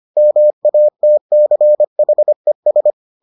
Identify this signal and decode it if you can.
Morse code, heard as a keyed tone, about 25 wpm, 600 Hz. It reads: MATCHES